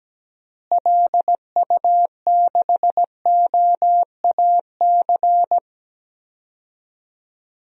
Morse code, heard as a keyed tone, about 17 wpm, 700 Hz. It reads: LU6OAC
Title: DXCC-100 Call Signs - 6 Characters - Repeated 2x - 17wpm